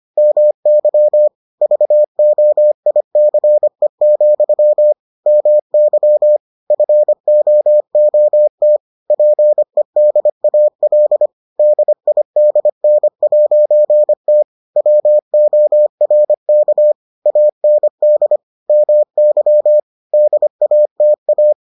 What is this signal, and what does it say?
Morse code, heard as a keyed tone, about 25 wpm, 600 Hz. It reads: MY VOICE, MY FOOT PEDAL DIDN'T WORK AND MY DATA